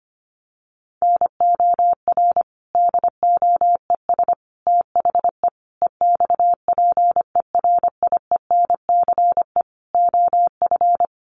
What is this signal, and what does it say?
Morse code, heard as a keyed tone, about 25 wpm, 700 Hz. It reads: NOR DOES THE EXPERIENCE OF
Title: Sentences from Top 1500 Words 25wpm